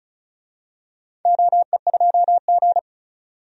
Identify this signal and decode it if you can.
Morse code, heard as a keyed tone, about 34 wpm, 700 Hz. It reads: OE2G
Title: DXCC-100 Call Signs - 4 Characters - 35wpm